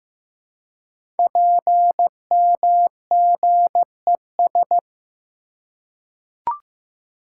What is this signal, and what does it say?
Morse code, heard as a keyed tone, about 15 wpm, 700 Hz. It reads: PMGES
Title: Code Groups - 5 Letters - 15wpm